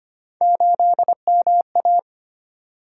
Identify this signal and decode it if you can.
Morse code, heard as a keyed tone, about 25 wpm, 700 Hz. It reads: 8MA
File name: Three Letter-Number Combinations - 25wpm